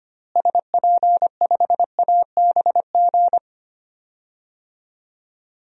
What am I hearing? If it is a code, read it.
Morse code, heard as a keyed tone, about 25 wpm, 700 Hz. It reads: SP5ABG
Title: DXCC-100 Call Signs - 6 Characters - Repeated 2x - 25wpm